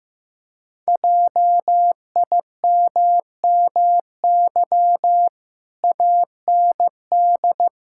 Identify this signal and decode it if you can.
Morse code, heard as a keyed tone, about 14 wpm, 700 Hz. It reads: JIMMY AND
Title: Pangram Challenge v2 - 15wpm